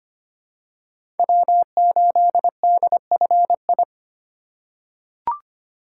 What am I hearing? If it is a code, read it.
Morse code, heard as a keyed tone, about 25 wpm, 700 Hz. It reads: W8DFI
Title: USA Call Signs - 3x Character Spacing - 25wpm